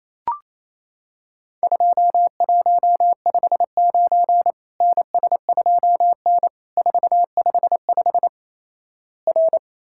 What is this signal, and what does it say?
Morse code, heard as a keyed tone, about 28 wpm, 700 Hz. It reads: 2159 NS2N 455 R
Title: Worked All Europe DX Contest - QTC Exchange - 28wpm